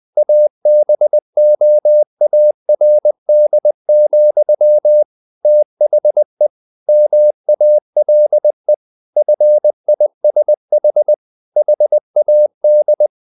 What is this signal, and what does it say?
Morse code, heard as a keyed tone, about 20 wpm, 600 Hz. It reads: ABOARD, THE MALE FISH HAD
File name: the old man and the sea / the old man and the sea part 1 20wpm